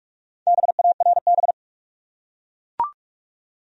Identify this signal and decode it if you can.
Morse code, heard as a keyed tone, about 42 wpm, 700 Hz. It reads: BARB
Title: USA Common Names - Max 5 Letters - 45wpm